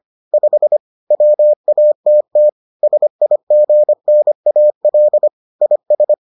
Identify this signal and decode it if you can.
Morse code, heard as a keyed tone, about 25 wpm, 600 Hz. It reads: 5 WATT SIGNAL IS